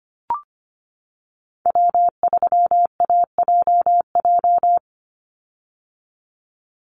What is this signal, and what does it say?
Morse code, heard as a keyed tone, about 25 wpm, 700 Hz. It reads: W3AJJ
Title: USA Call Signs 25wpm